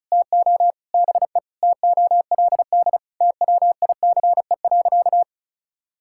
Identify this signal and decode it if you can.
Morse code, heard as a keyed tone, about 34 wpm, 700 Hz. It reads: TO BE TOLD TWICE.
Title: Sentences from Top 1600 Words 35wpm